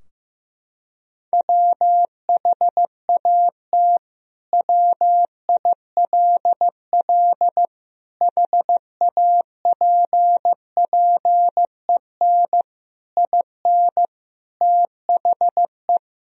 Morse code, heard as a keyed tone, about 16 wpm, 700 Hz. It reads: WHAT WILL HAPPEN IN THE
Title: Sentences from Top 1000 Words - Review Part 2 - 15wpm